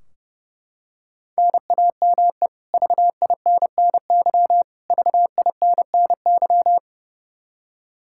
Morse code, heard as a keyed tone, about 30 wpm, 700 Hz. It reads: NAME VINNY VINNY
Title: Practice QSO - Exchange 1 - Beginning - 30wpm